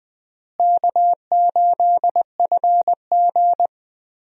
Morse code, heard as a keyed tone, about 20 wpm, 700 Hz. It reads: K8FG
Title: USA Call Signs - 4 Characters - Repeated 3x - 20wpm